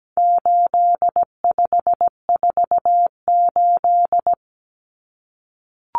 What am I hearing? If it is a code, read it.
Morse code, heard as a keyed tone, about 17 wpm, 700 Hz. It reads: 8548
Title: Numbers - 4-Digit 17wpm